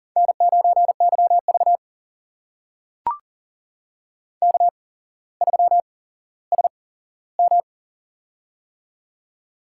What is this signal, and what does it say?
Morse code, heard as a keyed tone, about 40 wpm, spaced as for 17 wpm, 700 Hz. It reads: N9YV K3SM
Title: USA Call Signs - 4 Characters - 8x Character Spacing - 40wpm